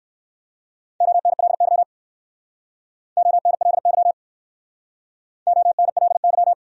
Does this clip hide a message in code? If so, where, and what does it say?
Yes: Morse code from 1.0 to 6.5 s, heard as a keyed tone, about 60 wpm, 700 Hz. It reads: ONLY ONLY ONLY